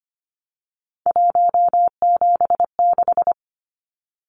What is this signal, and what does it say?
Morse code, heard as a keyed tone, about 25 wpm, 700 Hz. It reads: 176
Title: Numbers - 3-Digit 25wpm